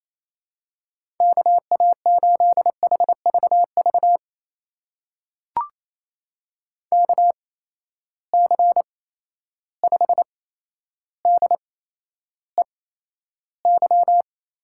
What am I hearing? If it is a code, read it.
Morse code, heard as a keyed tone, about 28 wpm, spaced as for 12 wpm, 700 Hz. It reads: KA8HVV KC5DEY
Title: USA Call Signs - 8x Character Spacing - 28wpm